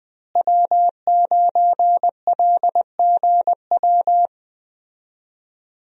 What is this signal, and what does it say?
Morse code, heard as a keyed tone, about 20 wpm, 700 Hz. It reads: W9LGW
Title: USA Call Signs - 5 Characters - Repeated 2x - 20wpm